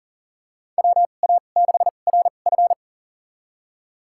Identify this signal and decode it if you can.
Morse code, heard as a keyed tone, about 39 wpm, spaced as for 29 wpm, 700 Hz. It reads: WA6RF
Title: USA Call Signs - 5 Characters - 2x Character Spacing - 40wpm